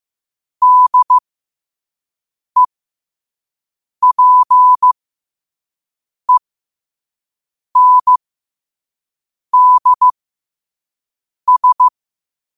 Morse code, heard as a keyed tone, about 15 wpm, spaced as for 5 wpm, 1000 Hz. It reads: DEPENDS
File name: qotdfiles / 5wpm